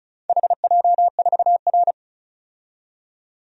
Morse code, heard as a keyed tone, about 36 wpm, 700 Hz. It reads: HJ4R
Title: DXCC-100 Call Signs - 4 Characters - 35wpm